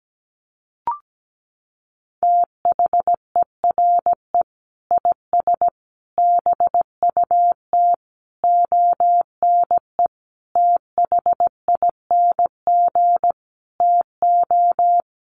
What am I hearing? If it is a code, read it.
Morse code, heard as a keyed tone, about 17 wpm, 700 Hz. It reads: THERE IS BUT ONE THING TO
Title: Sentences from Top 100 Words 17wpm